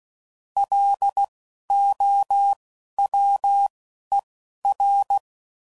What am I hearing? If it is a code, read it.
Morse code, heard as a keyed tone, about 16 wpm, spaced as for 11 wpm, 800 Hz. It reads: LOWER
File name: CW Code Groups @10WPM / Technical Subjects / XC